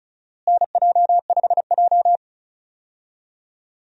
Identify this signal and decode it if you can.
Morse code, heard as a keyed tone, about 34 wpm, 700 Hz. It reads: NJ5J
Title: USA Call Signs - 4 Characters - Repeated 3x - 35wpm